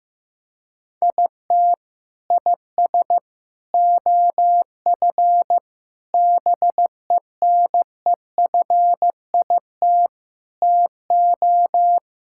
Morse code, heard as a keyed tone, about 15 wpm, 700 Hz. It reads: IT IS OF BENEFIT TO
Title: Sentences from Top 1000 Words - Review Part 6 - 15wpm